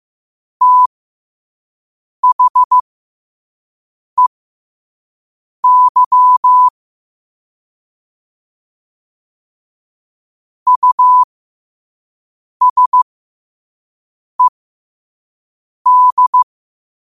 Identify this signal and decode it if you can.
Morse code, heard as a keyed tone, about 15 wpm, spaced as for 5 wpm, 1000 Hz. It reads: THEY USED
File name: qotdfiles / 5wpm